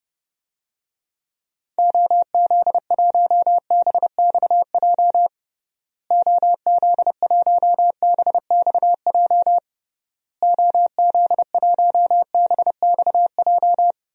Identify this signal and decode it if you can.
Morse code, heard as a keyed tone, about 30 wpm, 700 Hz. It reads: OZ1BXJ OZ1BXJ OZ1BXJ
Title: DXCC-100 Call Signs - 6 Characters - Repeated 3x - 30wpm